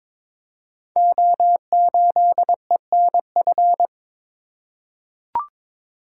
Morse code, heard as a keyed tone, about 22 wpm, 700 Hz. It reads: O8ENF
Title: Code Groups - 5 Letter:Numbers - 22wpm